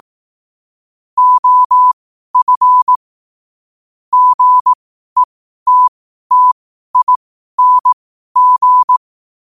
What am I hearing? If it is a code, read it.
Morse code, heard as a keyed tone, about 18 wpm, spaced as for 12 wpm, 1000 Hz. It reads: OF GETTING
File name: qotdfiles / 12wpm